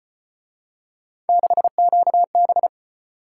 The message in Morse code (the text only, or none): 6QB